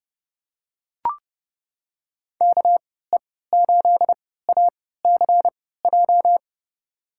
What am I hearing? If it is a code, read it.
Morse code, heard as a keyed tone, about 31 wpm, spaced as for 17 wpm, 700 Hz. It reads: KE8ACJ
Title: USA Call Signs - 6 Characters - 3x Character Spacing - 30wpm